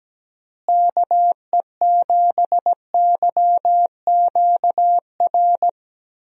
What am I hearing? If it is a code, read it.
Morse code, heard as a keyed tone, about 17 wpm, 700 Hz. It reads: KE7YQR
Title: USA Call Signs - 6 Characters - Repeated 2x - 17wpm